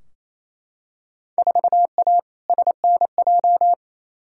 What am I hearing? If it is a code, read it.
Morse code, heard as a keyed tone, about 28 wpm, 700 Hz. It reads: 4A SNJ
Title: Field Day Exchanges Encore - 28wpm